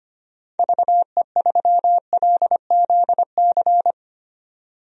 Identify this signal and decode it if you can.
Morse code, heard as a keyed tone, about 25 wpm, 700 Hz. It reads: VE3LZC
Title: DXCC-100 Call Signs - 6 Characters - 25wpm